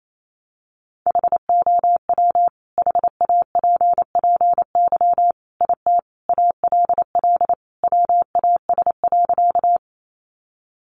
Morse code, heard as a keyed tone, about 28 wpm, 700 Hz. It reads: HOW HAPPY IT ALL WAS.
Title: Sentences from Top 1000 Words 28wpm